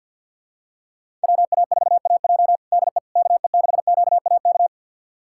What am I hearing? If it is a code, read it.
Morse code, heard as a keyed tone, about 50 wpm, 700 Hz. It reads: WA4AJ DE KE6QAK
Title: USA Callsign DE Callsign 50wpm